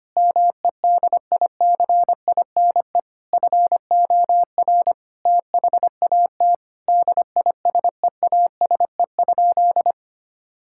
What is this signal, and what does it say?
Morse code, heard as a keyed tone, about 25 wpm, 700 Hz. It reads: MEDICINE FOR THAT DISEASE?
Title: Sentences from Top 1000 Words - Review Part 6 - 25wpm